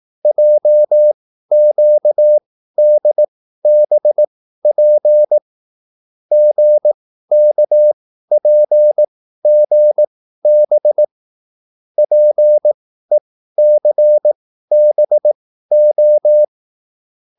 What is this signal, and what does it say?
Morse code, heard as a keyed tone, about 18 wpm, spaced as for 13 wpm, 600 Hz. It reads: JQDBP GKPGB PECBO
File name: QSL / CW lesson